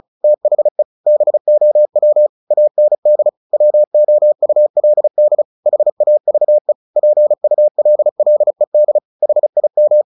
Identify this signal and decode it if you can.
Morse code, heard as a keyed tone, about 34 wpm, 600 Hz. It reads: THE BOW AND WOULD HAVE PULLED HIM
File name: the old man and the sea / the old man and the sea 1 35wpm